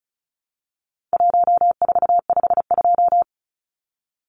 Morse code, heard as a keyed tone, about 34 wpm, 700 Hz. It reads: 1452
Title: Numbers - Contest Style - 4-Digit 35wpm